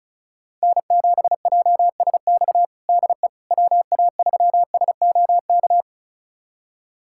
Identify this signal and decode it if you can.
Morse code, heard as a keyed tone, about 36 wpm, 690 Hz. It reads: N7JSX DE WA3SOK
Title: USA Callsign DE Callsign 35wpm